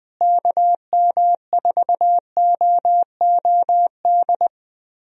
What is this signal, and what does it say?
Morse code, heard as a keyed tone, about 20 wpm, 700 Hz. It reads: KM4OOD
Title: USA Call Signs - 6 Characters - Repeated 3x - 20wpm